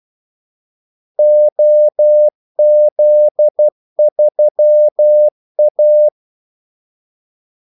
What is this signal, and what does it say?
Morse code, heard as a keyed tone, about 12 wpm, 600 Hz. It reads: OZ3A
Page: 2 ekstra ordmellemrum mellem kaldesignalerne